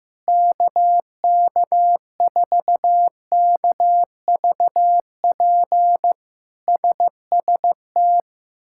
Morse code, heard as a keyed tone, about 15 wpm, 700 Hz. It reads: KK4KVP SST
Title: K1USN SST QSO Practice - 15wpm